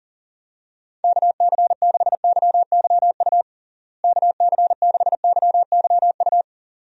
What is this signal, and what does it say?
Morse code, heard as a keyed tone, about 40 wpm, 700 Hz. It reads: KC6YYU KC6YYU
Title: USA Call Signs - 6 Characters - Repeated 2x - 40wpm